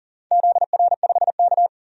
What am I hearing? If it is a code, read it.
Morse code, heard as a keyed tone, about 40 wpm, 700 Hz. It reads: ZR5K
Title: DXCC-100 Call Signs - 4 Characters - 40wpm